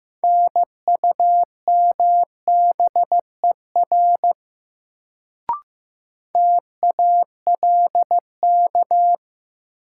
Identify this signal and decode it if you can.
Morse code, heard as a keyed tone, about 16 wpm, 700 Hz. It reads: NUMBER TALK
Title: Sets of 2 Words - New Word Drill - 15wpm